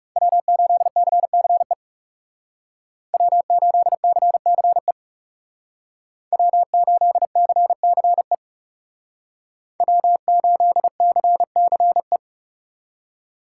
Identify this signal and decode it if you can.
Morse code, heard as a keyed tone, about 34 wpm, 700 Hz. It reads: W8CCE W8CCE W8CCE W8CCE
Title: Speed Racing - USA 1x3 Call Signs 30wpm